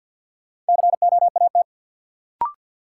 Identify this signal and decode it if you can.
Morse code, heard as a keyed tone, about 53 wpm, 700 Hz. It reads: COAT
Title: Sentences from Top 1600 Words - New Word Drill - 50wpm